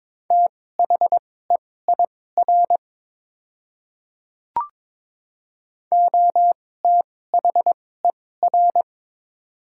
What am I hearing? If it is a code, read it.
Morse code, heard as a keyed tone, about 22 wpm, spaced as for 16 wpm, 700 Hz. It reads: THEIR OTHER
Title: 2x Character Spacing - Top 300 Words - 22wpm